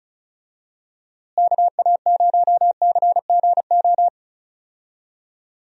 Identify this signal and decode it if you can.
Morse code, heard as a keyed tone, about 36 wpm, 700 Hz. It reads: KA0CGO
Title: USA Call Signs - 6 Characters - Repeated 3x - 35wpm